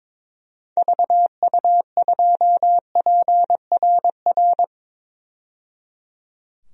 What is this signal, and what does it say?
Morse code, heard as a keyed tone, about 22 wpm, 700 Hz. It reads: VU2PRR
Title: DXCC-100 Call Signs - 6 Characters - 22wpm